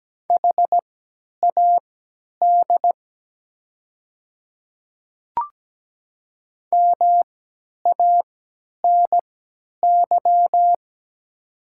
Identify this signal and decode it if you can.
Morse code, heard as a keyed tone, about 17 wpm, spaced as for 10 wpm, 700 Hz. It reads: HAD MANY
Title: 3x Character Spacing - Top 100 Words - 17wpm